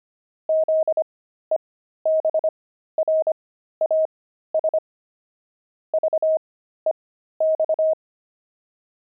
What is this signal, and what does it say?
Morse code, heard as a keyed tone, about 25 wpm, spaced as for 13 wpm, 630 Hz. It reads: ZEBRAS VEX
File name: pangram-fs-13-cs-25-daft